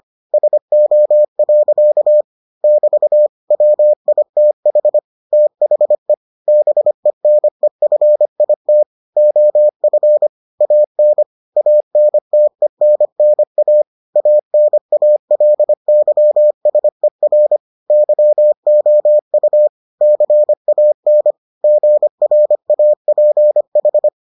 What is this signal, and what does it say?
Morse code, heard as a keyed tone, about 25 wpm, 600 Hz. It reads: SO. <BT> WITH THE BENEFIT OF AN ANTENNA ANALYSER YOU CAN GRAPH